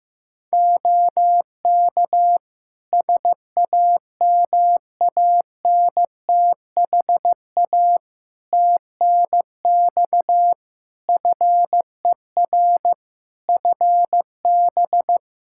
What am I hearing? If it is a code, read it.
Morse code, heard as a keyed tone, about 15 wpm, 700 Hz. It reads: OK SAMANTHA TNX FER FB
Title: Practice QSO - Exchange 4 - Ending - 15wpm